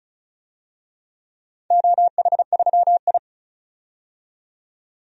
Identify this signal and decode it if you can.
Morse code, heard as a keyed tone, about 36 wpm, 700 Hz. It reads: OH3I
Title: DXCC-100 Call Signs - 4 Characters - 35wpm